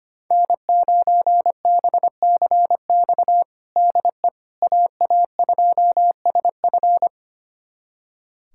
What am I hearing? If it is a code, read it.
Morse code, heard as a keyed tone, about 25 wpm, 700 Hz. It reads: N9BCX DE AA2SF
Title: USA Callsign DE Callsign - 5 Characters - 25wpm